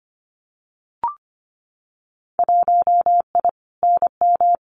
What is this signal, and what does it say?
Morse code, heard as a keyed tone, about 25 wpm, 700 Hz. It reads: 1I NM
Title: Winter Field Day Exchanges Encore 25wpm